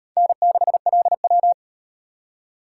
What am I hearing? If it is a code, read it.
Morse code, heard as a keyed tone, about 33 wpm, 700 Hz. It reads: N6LW